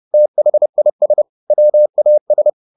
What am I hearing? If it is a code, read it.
Morse code, heard as a keyed tone, about 30 wpm, 600 Hz. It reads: THIS WAS
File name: the old man and the sea / the old man and the sea 1 30wpm